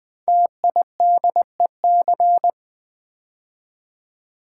Morse code, heard as a keyed tone, about 20 wpm, 700 Hz. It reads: TIDEC
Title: Code Groups - 5 Letters - 20wpm